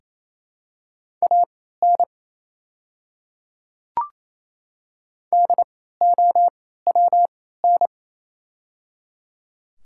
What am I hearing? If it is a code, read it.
Morse code, heard as a keyed tone, about 28 wpm, spaced as for 16 wpm, 700 Hz. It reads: AN DOWN
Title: 3x Character Spacing - Top 100 Words - 28wpm